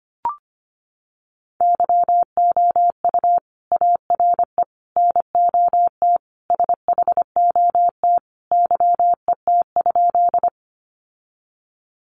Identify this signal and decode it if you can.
Morse code, heard as a keyed tone, about 25 wpm, 700 Hz. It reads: YOU ARE NOT SHOT YET?
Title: Sentences from Top 1100 Words 25wpm